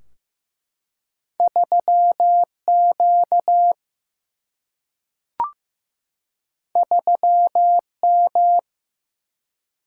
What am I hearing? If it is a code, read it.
Morse code, heard as a keyed tone, about 15 wpm, 700 Hz. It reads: 3Q 3M
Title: Single-Letter Single-Number 15wpm